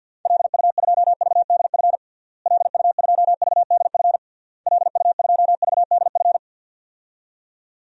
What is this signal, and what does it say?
Morse code, heard as a keyed tone, about 51 wpm, 700 Hz. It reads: LU2VDF LU2VDF LU2VDF
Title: DXCC-100 Call Signs - 6 Characters - Repeated 3x - 50wpm